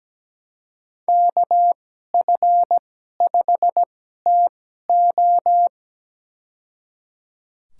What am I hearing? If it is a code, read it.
Morse code, heard as a keyed tone, about 17 wpm, spaced as for 12 wpm, 700 Hz. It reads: KF5TO
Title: USA Call Signs - 5 Characters - 2x Character Spacing - 17wpm